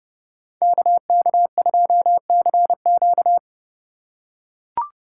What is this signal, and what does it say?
Morse code, heard as a keyed tone, about 32 wpm, 700 Hz. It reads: KK2CQ
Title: USA Call Signs - 5 Characters - 3x Character Spacing - 30wpm